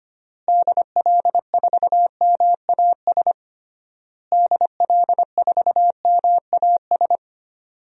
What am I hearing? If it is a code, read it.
Morse code, heard as a keyed tone, about 25 wpm, 700 Hz. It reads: DL4MAS DL4MAS
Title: DXCC-100 Call Signs - 6 Characters - Repeated 3x - 25wpm